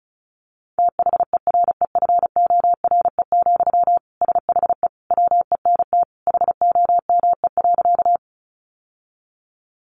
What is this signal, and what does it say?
Morse code, heard as a keyed tone, about 34 wpm, 700 Hz. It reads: THEREFORE, SHE WENT HOME.
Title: Sentences from Top 800 Words 35wpm